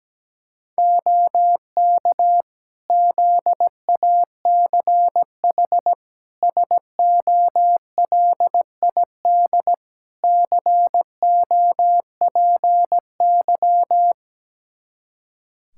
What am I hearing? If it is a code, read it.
Morse code, heard as a keyed tone, about 17 wpm, 700 Hz. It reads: OK ZACH SOLID COPY
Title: K1USN SST QSO Practice 17wpm